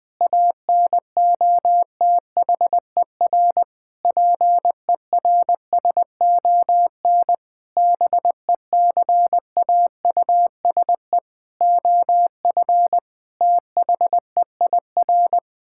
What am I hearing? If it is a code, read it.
Morse code, heard as a keyed tone, about 20 wpm, 700 Hz. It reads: ANOTHER PERSON BECAUSE OF THEIR
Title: Sentences with 9-Word Phrases - 20wpm